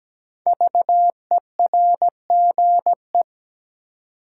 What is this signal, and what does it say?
Morse code, heard as a keyed tone, about 17 wpm, 700 Hz. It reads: VERGE
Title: Sets of 4 Words - New Word Drill - 17wpm